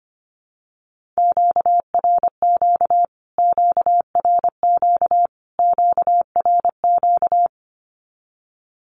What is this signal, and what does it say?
Morse code, heard as a keyed tone, about 25 wpm, 700 Hz. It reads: QRQ QRQ QRQ
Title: Q-Codes Repeated 3x 25wpm